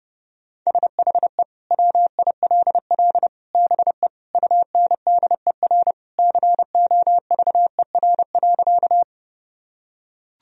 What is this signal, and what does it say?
Morse code, heard as a keyed tone, about 32 wpm, 700 Hz. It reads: SHE WILL BE UNDER COVER.
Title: Sentences from Top 1000 Words - Review Part 1 - 30wpm